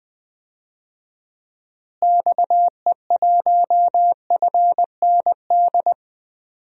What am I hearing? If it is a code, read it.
Morse code, heard as a keyed tone, about 20 wpm, 700 Hz. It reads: XE1FND
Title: DXCC-100 Call Signs - 6 Characters - Repeated 3x - 20wpm